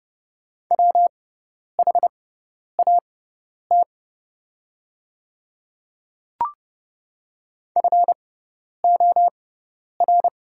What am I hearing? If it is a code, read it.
Morse code, heard as a keyed tone, about 30 wpm, spaced as for 10 wpm, 700 Hz. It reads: WHAT FOR
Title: 6x Character Spacing - Top 100 Words - 30wpm